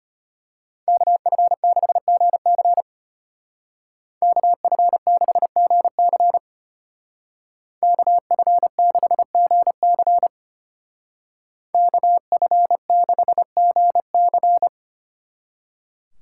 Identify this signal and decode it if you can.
Morse code, heard as a keyed tone, about 30 wpm, 700 Hz. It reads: KF6GC KF6GC KF6GC KF6GC
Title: Speed Racing - USA 2x2 Call Signs 25wpm